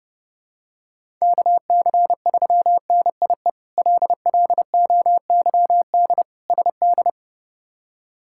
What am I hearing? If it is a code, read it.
Morse code, heard as a keyed tone, about 30 wpm, 700 Hz. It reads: KC3NIE LLOYD SD
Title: K1USN SST QSO Practice - 30wpm